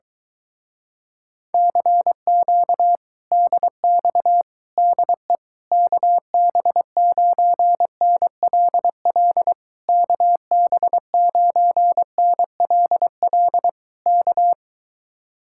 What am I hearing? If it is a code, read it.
Morse code, heard as a keyed tone, about 22 wpm, 700 Hz. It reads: CQ DX DE KB9NLL KB9NLL K